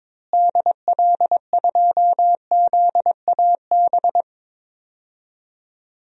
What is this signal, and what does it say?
Morse code, heard as a keyed tone, about 22 wpm, 700 Hz. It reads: DL2ZAB
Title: DXCC-100 Call Signs - 6 Characters - Repeated 3x - 22wpm